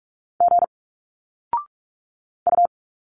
Morse code, heard as a keyed tone, about 46 wpm, 700 Hz. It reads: G U
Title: Single Letters 45wpm